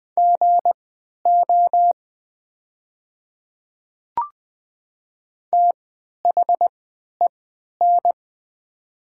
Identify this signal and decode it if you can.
Morse code, heard as a keyed tone, about 20 wpm, spaced as for 11 wpm, 700 Hz. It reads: GO THEN
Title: 3x Character Spacing - Top 100 Words - 20wpm